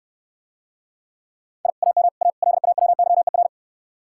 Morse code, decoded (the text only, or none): I AM A LAWYER